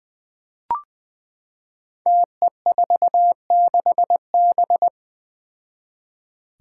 Morse code, heard as a keyed tone, about 21 wpm, 700 Hz. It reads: TE46B
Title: Five Challenging Letter-Number Combinations 20wpm